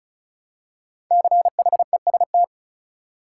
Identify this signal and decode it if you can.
Morse code, heard as a keyed tone, about 36 wpm, 700 Hz. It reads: CHEST
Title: Five Letter Words - 35wpm